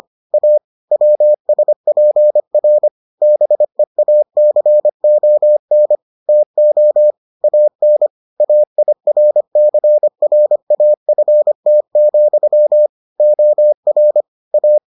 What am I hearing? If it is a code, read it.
Morse code, heard as a keyed tone, about 25 wpm, 600 Hz. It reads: A WSPR BEACON TO AN AIRCRAFT, OR A